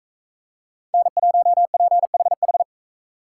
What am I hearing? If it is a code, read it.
Morse code, heard as a keyed tone, about 40 wpm, 700 Hz. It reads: N1PHH